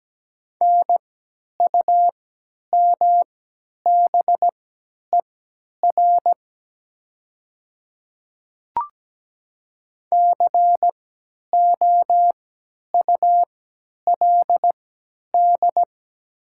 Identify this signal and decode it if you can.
Morse code, heard as a keyed tone, about 17 wpm, spaced as for 10 wpm, 690 Hz. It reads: NUMBER COULD
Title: 3x Character Spacing - Top 200 Words - 17wpm